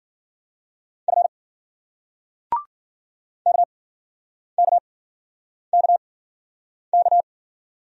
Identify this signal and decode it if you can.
Morse code, heard as a keyed tone, about 50 wpm, spaced as for 9 wpm, 700 Hz. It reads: F KKKK
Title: Speed-Racing Single Letters 40wpm